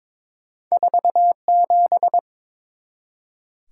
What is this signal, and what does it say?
Morse code, heard as a keyed tone, about 29 wpm, 700 Hz. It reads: 47